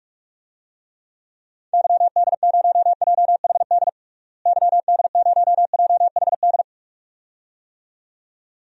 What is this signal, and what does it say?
Morse code, heard as a keyed tone, about 42 wpm, 700 Hz. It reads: YD0JHD YD0JHD
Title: DXCC-100 Call Signs - 6 Characters - Repeated 2x - 45wpm